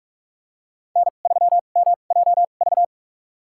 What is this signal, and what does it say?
Morse code, heard as a keyed tone, about 46 wpm, spaced as for 33 wpm, 700 Hz. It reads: N3MJV
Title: USA Call Signs - 2x Character Spacing - 45wpm